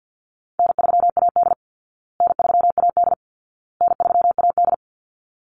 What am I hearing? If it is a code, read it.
Morse code, heard as a keyed tone, about 48 wpm, 700 Hz. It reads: N3AD N3AD N3AD
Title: Easy Call Signs Repeated 3x - 50wpm